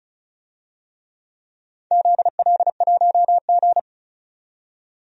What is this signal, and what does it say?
Morse code, heard as a keyed tone, about 36 wpm, 690 Hz. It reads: ZL1G
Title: DXCC-100 Call Signs - 4 Characters - 35wpm